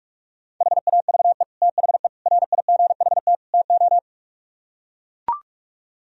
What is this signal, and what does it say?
Morse code, heard as a keyed tone, about 44 wpm, 700 Hz. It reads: HAVE THE RIGHT TO